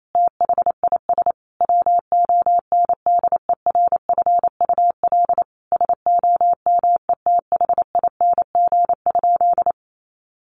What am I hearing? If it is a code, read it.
Morse code, heard as a keyed tone, about 28 wpm, 700 Hz. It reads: THIS WONDERFUL SOMETHING?
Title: Sentences from Top 1100 Words 28wpm